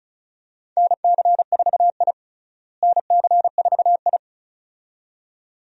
Morse code, heard as a keyed tone, about 36 wpm, 700 Hz. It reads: NC4I NC4I
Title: USA Call Signs - 4 Characters - Repeated 2x - 35wpm